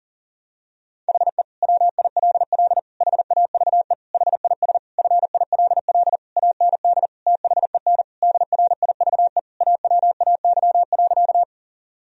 Morse code, heard as a keyed tone, about 37 wpm, 700 Hz. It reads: HE WILL HAVE HIS FILL AND THEN DRIVE AWAY.
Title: Sentences from Top 1000 Words - Review Part 3 - 40wpm